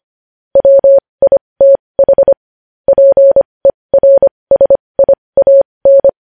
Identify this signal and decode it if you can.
Morse code, heard as a keyed tone, about 25 wpm, spaced as for 20 wpm, 550 Hz. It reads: WITH PERSIAN
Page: News headlines in morse code at 20 wpm